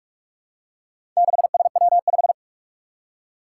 Morse code, heard as a keyed tone, about 44 wpm, 700 Hz. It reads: 6SW5
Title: Four Challenging Letter-Number Combinations 45wpm